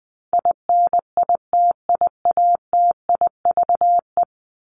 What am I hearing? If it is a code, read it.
Morse code, heard as a keyed tone, about 20 wpm, 700 Hz. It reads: INITIATIVE
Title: Top 1751-2000 Words 20wpm